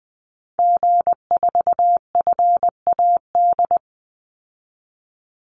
Morse code, heard as a keyed tone, about 20 wpm, 700 Hz. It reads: G4FAD
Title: Call Signs Repeated 3x - 20wpm